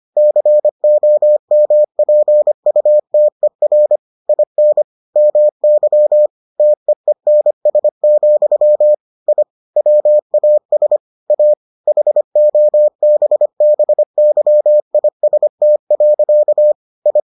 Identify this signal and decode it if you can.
Morse code, heard as a keyed tone, about 25 wpm, 600 Hz. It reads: COMPUTER IN MY TEENS, I WAS A HOBBYIST. I